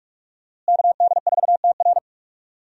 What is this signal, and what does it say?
Morse code, heard as a keyed tone, about 44 wpm, 700 Hz. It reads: KD4TR
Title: USA Call Signs - 5 Characters - Repeated 3x - 45wpm